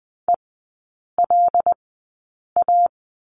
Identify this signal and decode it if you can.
Morse code, heard as a keyed tone, about 20 wpm, spaced as for 8 wpm, 700 Hz. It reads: ELA
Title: 074 Lesson 13e - Letter L - Three Characters 20wpm